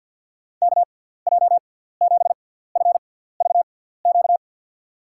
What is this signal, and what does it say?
Morse code, heard as a keyed tone, about 53 wpm, spaced as for 17 wpm, 700 Hz. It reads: KJ7FVQ
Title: USA Call Signs - 6x Character Spacing - 50wpm